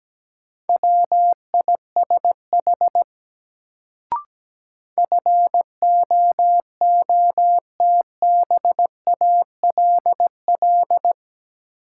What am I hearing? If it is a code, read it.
Morse code, heard as a keyed tone, about 17 wpm, 700 Hz. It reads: WISH FOOTBALL
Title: Sentences from Top 1600 Words - New Word Drill - 17wpm